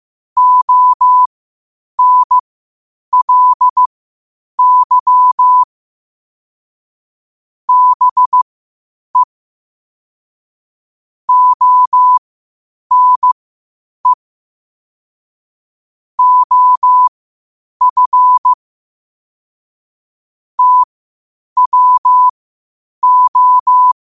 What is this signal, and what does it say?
Morse code, heard as a keyed tone, about 15 wpm, spaced as for 8 wpm, 1000 Hz. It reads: ONLY BE ONE OF TWO
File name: qotdfiles / 8wpm